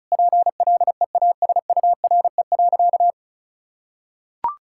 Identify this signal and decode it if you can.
Morse code, heard as a keyed tone, about 36 wpm, 700 Hz. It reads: PLEASURE.
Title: Sentences from Top 1600 Words 35wpm